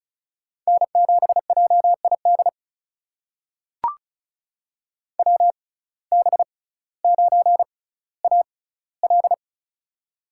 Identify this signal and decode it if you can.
Morse code, heard as a keyed tone, about 36 wpm, spaced as for 18 wpm, 700 Hz. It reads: N7JID WB9AL
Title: USA Call Signs - 6x Character Spacing - 35wpm